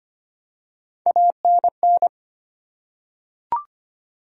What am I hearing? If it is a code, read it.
Morse code, heard as a keyed tone, about 25 wpm, 700 Hz. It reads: ANN
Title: USA Common Names - Max 3 Letters - 25wpm